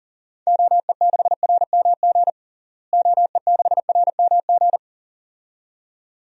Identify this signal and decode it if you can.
Morse code, heard as a keyed tone, about 40 wpm, 700 Hz. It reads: OE6RMG OE6RMG
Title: DXCC-100 Call Signs - 6 Characters - Repeated 2x - 40wpm